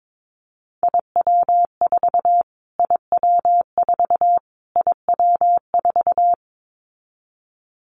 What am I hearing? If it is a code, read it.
Morse code, heard as a keyed tone, about 22 wpm, 700 Hz. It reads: IW4 IW4 IW4
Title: Callsign Prefixes for Top 25 DXCC - Repeated 3x - 22wpm